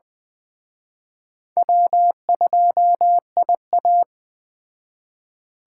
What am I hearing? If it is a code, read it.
Morse code, heard as a keyed tone, about 20 wpm, 700 Hz. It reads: W2IA
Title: Speed Racing - USA 1x2 Call Signs 17wpm